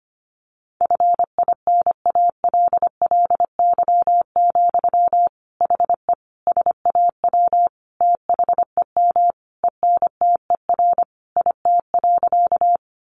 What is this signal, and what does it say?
Morse code, heard as a keyed tone, about 25 wpm, 700 Hz. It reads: FINALLY, HE SAW THEM ENTER IT.
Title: Sentences from Top 1300 Words 25wpm